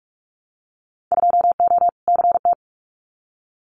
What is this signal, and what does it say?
Morse code, heard as a keyed tone, about 46 wpm, 700 Hz. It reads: 2O CT
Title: Winter Field Day Exchanges 45wpm